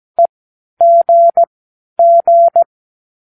Morse code, heard as a keyed tone, about 17 wpm, spaced as for 10 wpm, 680 Hz. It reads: EGG
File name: k7qo / Practicewords